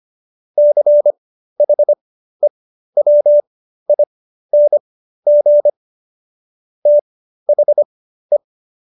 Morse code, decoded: CHEWING THE

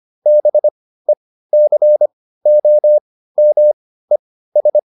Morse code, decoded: BECOMES

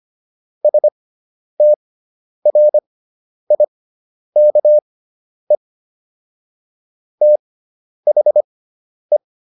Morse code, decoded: STRIKE THE

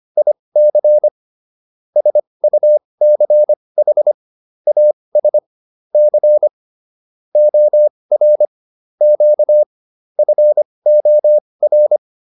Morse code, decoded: IC SUCH AS C OR Q FOR

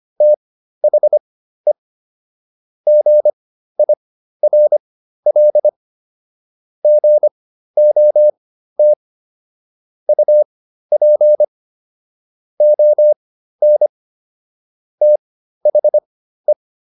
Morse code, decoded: THE GIRL GOT UP ON THE

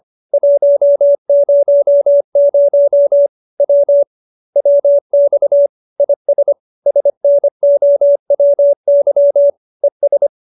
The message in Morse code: 100 W WX IS SNOWY ES

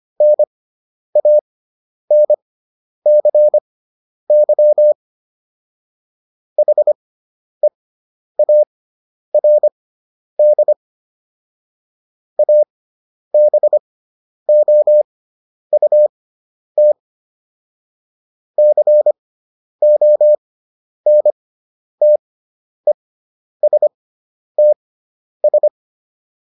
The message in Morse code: NANCY HEARD ABOUT CONTESTS

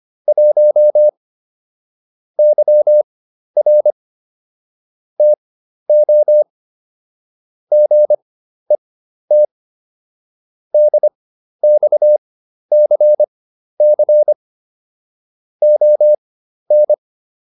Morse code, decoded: 1 YR TO GET DXCC ON

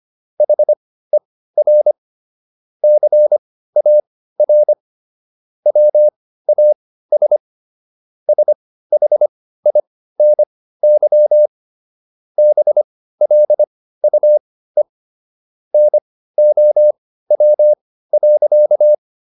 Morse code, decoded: HER CAR WAS SHINY BLUE NOW.

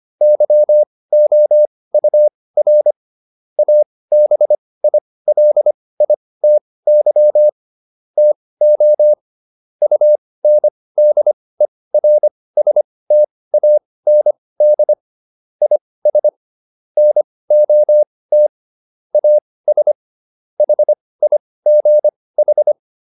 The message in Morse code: YOUR ABILITY TO UNDERSTAND IS NOT AS HIGH

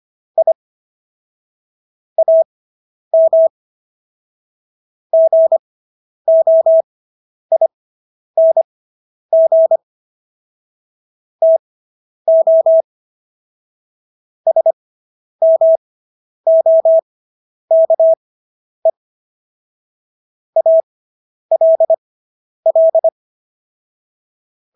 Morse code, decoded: I AM GOING TO SMOKE ALL

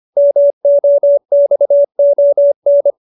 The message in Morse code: MOXON

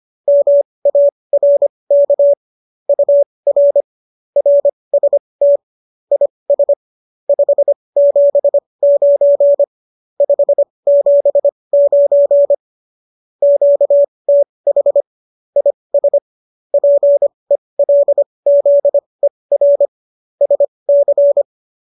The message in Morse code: MARK UR RST IS 579 579 QTH IS PELZER SC

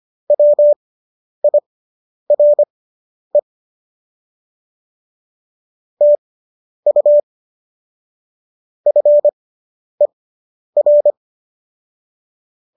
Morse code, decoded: WIRE TU FER

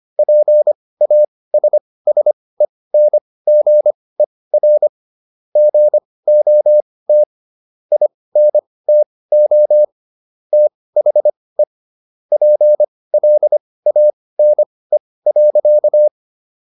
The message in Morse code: PASSENGER GOT INTO THE PLANE.

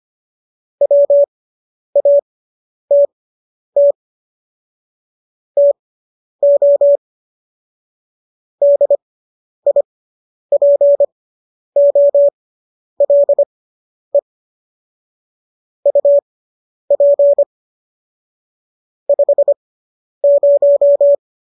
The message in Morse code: WATT TO DIPOLE UP 50

